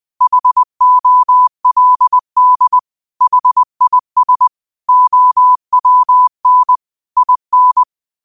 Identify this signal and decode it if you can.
Morse code, heard as a keyed tone, about 20 wpm, 1000 Hz. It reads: HOLD HIS OWN IN